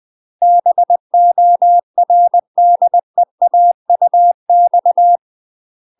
Morse code, heard as a keyed tone, about 20 wpm, 700 Hz. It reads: BORDEAUX
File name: UK MASTER QSOs TRAINING / 20WPM / UK MASTER QSOs 0001